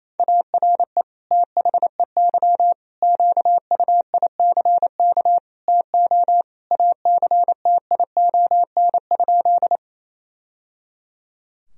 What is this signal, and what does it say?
Morse code, heard as a keyed tone, about 27 wpm, 700 Hz. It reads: ARE THEY QUICK TO ACTION?
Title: Sentences from Top 1000 Words - Review Part 4 - 28wpm